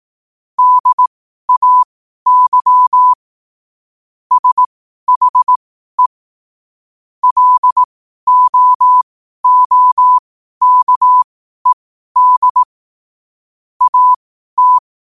Morse code, heard as a keyed tone, about 18 wpm, spaced as for 12 wpm, 1000 Hz. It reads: DAY SHE LOOKED AT
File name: qotdfiles / 12wpm